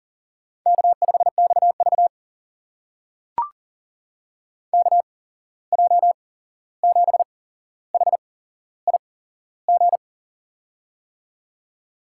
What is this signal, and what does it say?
Morse code, heard as a keyed tone, about 39 wpm, spaced as for 15 wpm, 700 Hz. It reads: K5XV KJ7HIG
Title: USA Call Signs - 8x Character Spacing - 40wpm